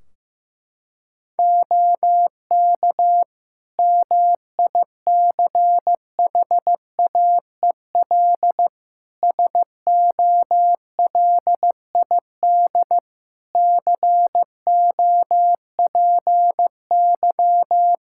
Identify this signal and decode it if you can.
Morse code, heard as a keyed tone, about 15 wpm, 700 Hz. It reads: OK MICHAEL SOLID COPY
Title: K1USN SST QSO Practice 15wpm